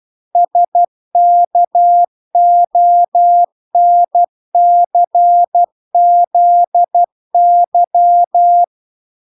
Morse code, heard as a keyed tone, about 12 wpm, 700 Hz. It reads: SKONCZY